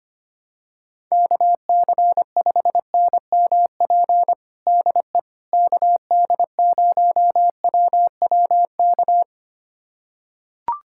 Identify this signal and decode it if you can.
Morse code, heard as a keyed tone, about 25 wpm, 700 Hz. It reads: KC5NMP DE KD0WWK
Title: USA Callsign DE Callsign 25wpm